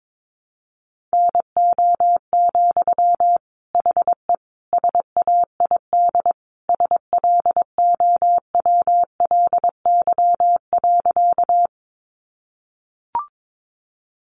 Morse code, heard as a keyed tone, about 22 wpm, 700 Hz. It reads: NO, HE SAID SLOWLY.
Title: Sentences from Top 1100 Words 22wpm